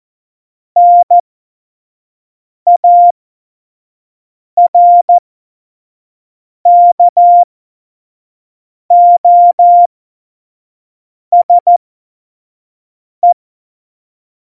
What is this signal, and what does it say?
Morse code, heard as a keyed tone, about 14 wpm, spaced as for 5 wpm, 700 Hz. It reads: NARKOSE
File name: Morsekurs / Witze / Witz 23. Januar 2018, 25 F70